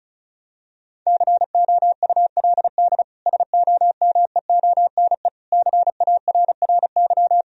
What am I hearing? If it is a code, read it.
Morse code, heard as a keyed tone, about 34 wpm, 700 Hz. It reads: COULD SOMEONE CARRY
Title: Sentences from Top 1000 Words - Review Part 4 - 35wpm